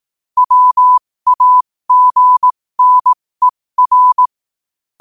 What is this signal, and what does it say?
Morse code, heard as a keyed tone, about 18 wpm, spaced as for 15 wpm, 1000 Hz. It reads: WAGNER
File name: qotdfiles / 15wpm